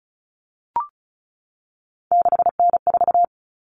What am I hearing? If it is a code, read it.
Morse code, heard as a keyed tone, about 34 wpm, 700 Hz. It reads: 6N4
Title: Common Cut Numbers - 3-Digit 35wpm